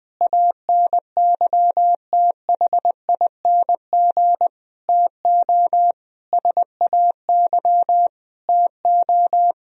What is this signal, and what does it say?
Morse code, heard as a keyed tone, about 20 wpm, 700 Hz. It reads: ANYTHING TO SAY TO
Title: Sentences from Top 1000 Words - Review Part 1 - 20wpm